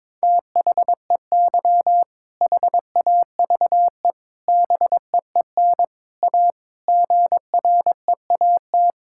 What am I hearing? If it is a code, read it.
Morse code, heard as a keyed tone, about 22 wpm, 700 Hz. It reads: THEY HAVE BEEN A GREAT
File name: practice / 22wpm / Sentences with 3-Word Phrases - 22wpm